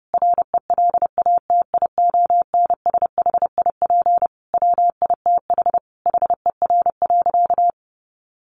Morse code, heard as a keyed tone, about 30 wpm, 700 Hz. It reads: RELATIONSHIP WITH HER.
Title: Sentences from Top 1100 Words 30wpm